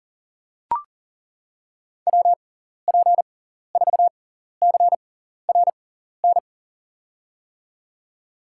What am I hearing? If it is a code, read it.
Morse code, heard as a keyed tone, about 40 wpm, spaced as for 14 wpm, 700 Hz. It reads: WP4CRN
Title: USA Call Signs - 6 Characters - 6x Character Spacing - 40wpm